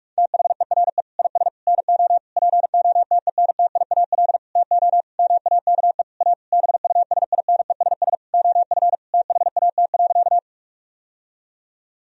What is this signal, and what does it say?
Morse code, heard as a keyed tone, about 46 wpm, 700 Hz. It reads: THERE IS NO POTENTIAL TO MAKE A BUSINESS OF THAT.
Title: Sentences from Top 1000 Words - Review Part 4 - 45wpm